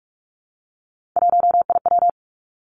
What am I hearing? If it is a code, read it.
Morse code, heard as a keyed tone, about 46 wpm, 700 Hz. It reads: 1IW
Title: Three Challenging Letter-Number Combinations 45wpm